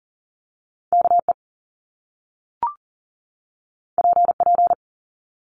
Morse code, heard as a keyed tone, about 39 wpm, 700 Hz. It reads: KE PP
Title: Top Two Letter Combinations in Words 40wpm